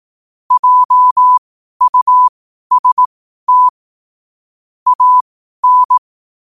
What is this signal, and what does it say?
Morse code, heard as a keyed tone, about 18 wpm, spaced as for 12 wpm, 1000 Hz. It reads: JUST AN
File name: qotdfiles / 12wpm